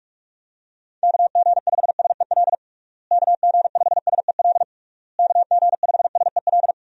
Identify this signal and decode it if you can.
Morse code, heard as a keyed tone, about 46 wpm, 700 Hz. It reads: KG5SEL KG5SEL KG5SEL
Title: USA Call Signs - 6 Characters - Repeated 3x - 45wpm